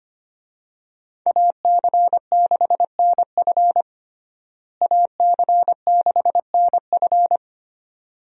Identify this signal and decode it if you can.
Morse code, heard as a keyed tone, about 25 wpm, 700 Hz. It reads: AC6NF AC6NF
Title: USA Call Signs - 5 Characters - Repeated 2x - 25wpm